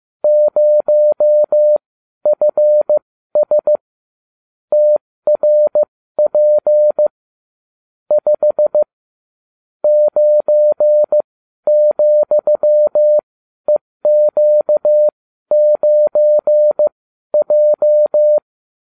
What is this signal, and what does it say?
Morse code, heard as a keyed tone, about 15 wpm, spaced as for 12 wpm, 600 Hz. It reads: 0FS TRP 5 9,EQ9J